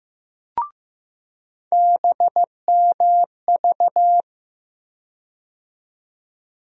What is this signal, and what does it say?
Morse code, heard as a keyed tone, about 15 wpm, 700 Hz. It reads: BMV
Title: USA Call Sign Postfixes - 3 Letters 15wpm